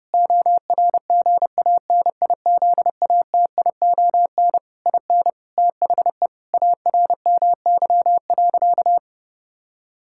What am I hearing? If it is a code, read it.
Morse code, heard as a keyed tone, about 30 wpm, 700 Hz. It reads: ORGANIZATION IN THE ARMY.
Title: Sentences from Top 1000 Words - Review Part 5 - 30wpm